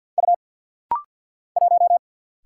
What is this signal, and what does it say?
Morse code, heard as a keyed tone, about 50 wpm, 700 Hz. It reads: U 1
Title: Single Letter-Number Alt - Rapid-Fire 50wpm